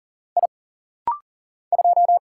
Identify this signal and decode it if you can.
Morse code, heard as a keyed tone, about 40 wpm, 700 Hz. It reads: I 2
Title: Single Letter-Number - Rapid-Fire 40wpm